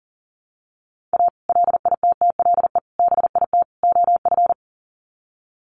Morse code, heard as a keyed tone, about 40 wpm, 700 Hz. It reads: A LITTLE BIT OF
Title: Sets of 4 Words 40wpm